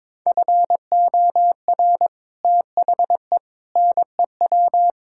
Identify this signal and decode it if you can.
Morse code, heard as a keyed tone, about 22 wpm, 700 Hz. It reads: FOR THE NEW
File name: practice / 22wpm / Build A Phrase - 5-Word Phrases - 22wpm